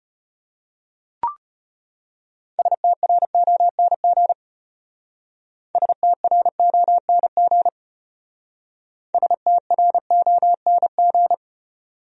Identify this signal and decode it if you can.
Morse code, heard as a keyed tone, about 32 wpm, 700 Hz. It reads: STRONG STRONG STRONG